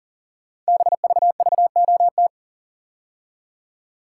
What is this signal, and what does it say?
Morse code, heard as a keyed tone, about 39 wpm, 700 Hz. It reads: BVVOT
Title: Code Groups - 5 Letter:Numbers - 40wpm